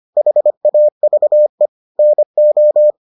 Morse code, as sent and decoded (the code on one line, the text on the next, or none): .... .- ...- . / -. ---
HAVE NO